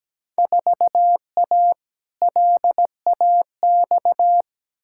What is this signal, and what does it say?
Morse code, heard as a keyed tone, about 17 wpm, 700 Hz. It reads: 4A LAX
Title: Field Day Exchanges - 17wpm